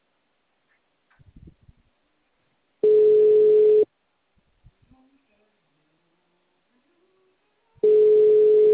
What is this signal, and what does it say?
Ringback tone (Europe, 425 Hz).
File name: out- -2012- - - .52037